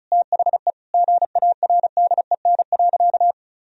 Morse code, heard as a keyed tone, about 36 wpm, 700 Hz. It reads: THE GARDEN.
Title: Sentences from Top 1000 Words - Review Part 4 - 35wpm